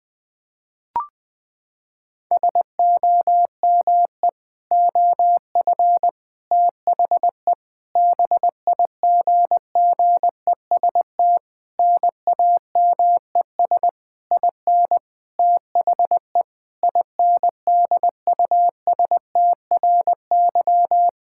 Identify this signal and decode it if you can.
Morse code, heard as a keyed tone, about 20 wpm, 700 Hz. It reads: SOME OF THE BIGGEST NAMES IN THE INDUSTRY
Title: Sentences with 7-Word Phrases - 20wpm